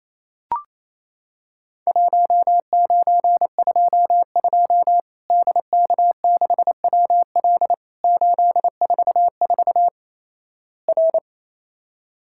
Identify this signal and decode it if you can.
Morse code, heard as a keyed tone, about 27 wpm, 700 Hz. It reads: 1922 DK6WL 844 R
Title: Worked All Europe DX Contest - QTC Exchange - 28wpm